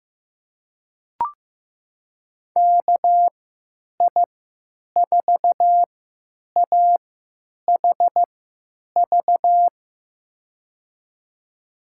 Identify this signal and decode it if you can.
Morse code, heard as a keyed tone, about 15 wpm, spaced as for 9 wpm, 700 Hz. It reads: KI4AHV
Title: USA Call Signs - 6 Characters - 3x Character Spacing - 15wpm